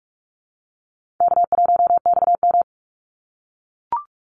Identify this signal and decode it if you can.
Morse code, heard as a keyed tone, about 44 wpm, 700 Hz. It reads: K1XM
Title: Easy Call Signs - 45wpm